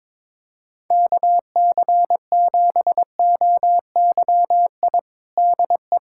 Morse code, heard as a keyed tone, about 22 wpm, 700 Hz. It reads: KC7OYI DE
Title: USA Callsign DE Callsign - 6 Characters - 22wpm